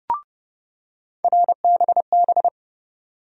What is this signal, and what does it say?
Morse code, heard as a keyed tone, about 30 wpm, 700 Hz. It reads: RBB
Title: USA Call Sign Postfixes - 3 Letters 30wpm